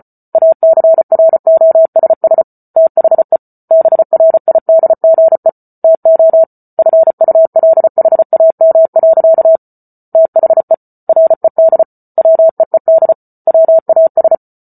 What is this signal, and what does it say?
Morse code, heard as a keyed tone, about 34 wpm, 650 Hz. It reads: ACROSS THE BRIDGE TO FULHAM. THE RED WEED WAS